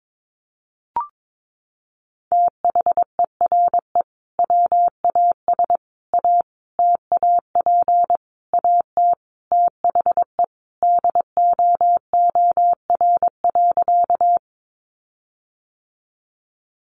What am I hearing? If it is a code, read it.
Morse code, heard as a keyed tone, about 22 wpm, 700 Hz. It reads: THERE WAS A TAP AT THE DOOR.
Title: Sentences from Top 1400 Words 22wpm